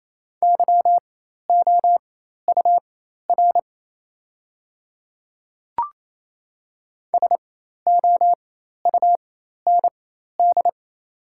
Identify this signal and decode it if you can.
Morse code, heard as a keyed tone, about 28 wpm, spaced as for 13 wpm, 700 Hz. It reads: YOUR SOUND
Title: 4x Character Spacing - Top 100 Words - 28wpm